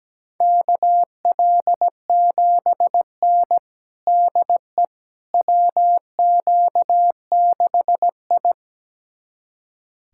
Morse code, heard as a keyed tone, about 17 wpm, 700 Hz. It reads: KL7N DE WQ6I
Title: USA Callsign DE Callsign - 4 Characters - 17wpm